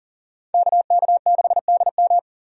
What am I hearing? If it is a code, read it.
Morse code, heard as a keyed tone, about 40 wpm, 700 Hz. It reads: KK6DM
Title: USA Call Signs - 5 Characters - 40wpm